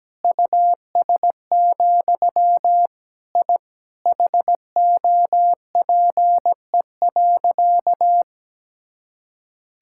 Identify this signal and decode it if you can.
Morse code, heard as a keyed tone, about 17 wpm, 700 Hz. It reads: US, I HOPE.
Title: Sentences from Top 1000 Words - Review Part 6 - 17wpm